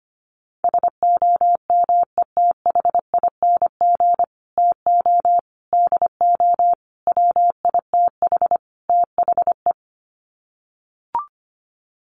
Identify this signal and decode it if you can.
Morse code, heard as a keyed tone, about 25 wpm, 690 Hz. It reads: SOMETHING TO DO WITH THE
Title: Sets of 5 Words Encore 25wpm